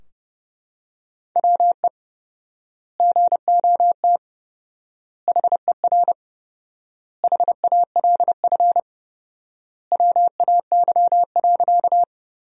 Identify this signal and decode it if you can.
Morse code, heard as a keyed tone, about 30 wpm, 700 Hz. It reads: WE GOT HER HALF WAY.
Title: 4x Word-Spacing - Sentences from Top 400 Words - 30wpm